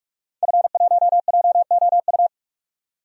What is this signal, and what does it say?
Morse code, heard as a keyed tone, about 46 wpm, 700 Hz. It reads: F1JOU
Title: DXCC-100 Call Signs - 5 Characters - 45wpm